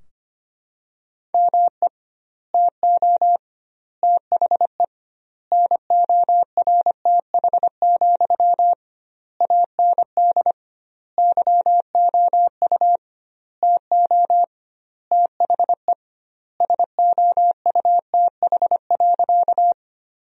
Morse code, heard as a keyed tone, about 25 wpm, 700 Hz. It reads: ME TO THE NORTH, AND YOU TO THE SOUTH.
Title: 2x Word-Spacing - Sentences from Top 400 Words - 25wpm